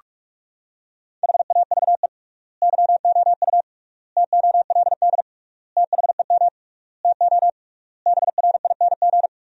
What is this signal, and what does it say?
Morse code, heard as a keyed tone, about 46 wpm, 700 Hz. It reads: HAVE YOU TOLD THEM TO BRING
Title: 3x Word-Spacing - Sentences from Top 500 Words - 45wpm